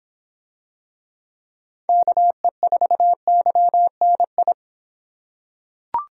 Morse code, heard as a keyed tone, about 25 wpm, 700 Hz. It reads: KE4YNI